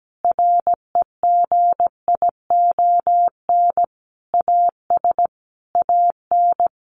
Morse code, heard as a keyed tone, about 17 wpm, 700 Hz. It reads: REGION AS AN
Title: Sentences from Top 1500 Words 17wpm